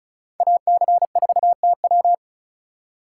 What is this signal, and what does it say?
Morse code, heard as a keyed tone, about 44 wpm, 700 Hz. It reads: AC4TW